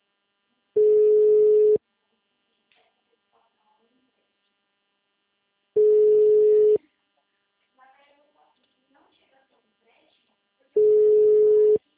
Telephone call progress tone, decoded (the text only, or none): ringback tone